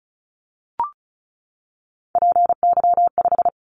Morse code, heard as a keyed tone, about 36 wpm, 690 Hz. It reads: PY5
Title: Callsign Prefixes for Top 50 DXCC - 35wpm